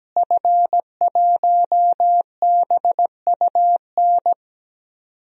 Morse code, heard as a keyed tone, about 17 wpm, 700 Hz. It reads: F1BUN
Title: DXCC-100 Call Signs - 5 Characters - Repeated 3x - 17wpm